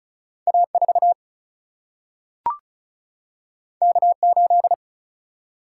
Morse code, heard as a keyed tone, about 36 wpm, 700 Hz. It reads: A 4 K 8
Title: Single-Letter Single-Number 35wpm